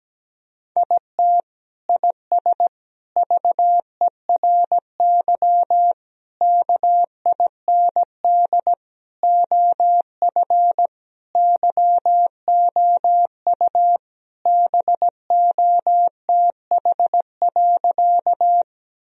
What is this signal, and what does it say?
Morse code, heard as a keyed tone, about 17 wpm, 700 Hz. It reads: IT IS VERY KIND OF YOU BOTH.
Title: Sentences from Top 1000 Words - Review Part 1 - 17wpm